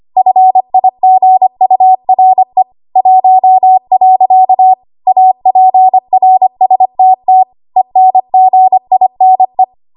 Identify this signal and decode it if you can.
Morse code, heard as a keyed tone, about 25 wpm, 750 Hz. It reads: FIGURE 1. APRSTT ENGINE
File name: Morse / 25WPM